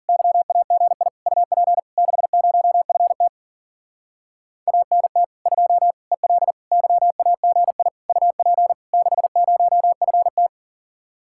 Morse code, heard as a keyed tone, about 51 wpm, 700 Hz. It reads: YAGI UP 60FT ANT 2 EL YAGI UP 60FT